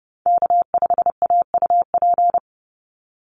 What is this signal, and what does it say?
Morse code, heard as a keyed tone, about 30 wpm, 700 Hz. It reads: K5AUP
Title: Call Signs Repeated 2x - 30wpm